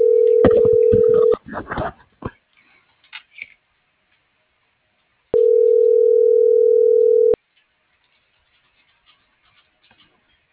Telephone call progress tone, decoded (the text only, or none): ringback tone